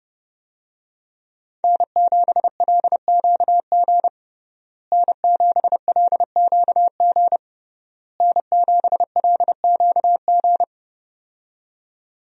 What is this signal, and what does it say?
Morse code, heard as a keyed tone, about 31 wpm, 700 Hz. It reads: N7LQG N7LQG N7LQG
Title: USA Call Signs - 5 Characters - Repeated 3x - 30wpm